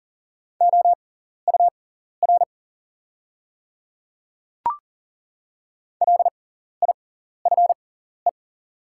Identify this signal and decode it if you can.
Morse code, heard as a keyed tone, about 40 wpm, spaced as for 14 wpm, 700 Hz. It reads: OUR LIFE
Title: 6x Character Spacing - Top 300 Words - 40wpm